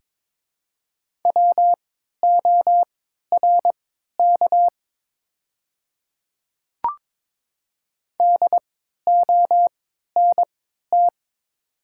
Morse code, heard as a keyed tone, about 22 wpm, spaced as for 12 wpm, 700 Hz. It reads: WORK DONT
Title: 3x Character Spacing - Top 300 Words - 22wpm